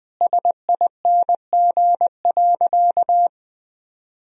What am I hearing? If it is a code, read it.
Morse code, heard as a keyed tone, about 20 wpm, 700 Hz. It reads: SING.
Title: Sentences from Top 1000 Words - Review Part 2 - 20wpm